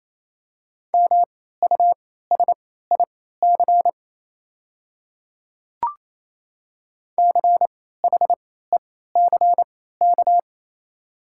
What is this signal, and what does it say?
Morse code, heard as a keyed tone, about 28 wpm, spaced as for 16 wpm, 690 Hz. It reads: MUSIC CHECK
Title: 3x Character Spacing - Top 500 Words - 28wpm